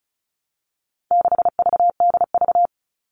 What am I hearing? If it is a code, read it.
Morse code, heard as a keyed tone, about 34 wpm, 700 Hz. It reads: 6VDV
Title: All Cut Numbers - 4-Digit 35wpm